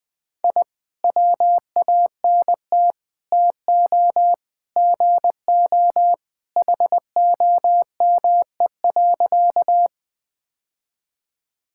Morse code, heard as a keyed tone, about 20 wpm, 700 Hz. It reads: I WANT TO GO HOME.
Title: Sentences with 3-Word Phrases - 20wpm